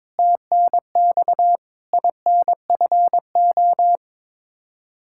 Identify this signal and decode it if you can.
Morse code, heard as a keyed tone, about 22 wpm, 700 Hz. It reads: TNX INFO
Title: Practice QSO - Exchange 3 - More Info - 22wpm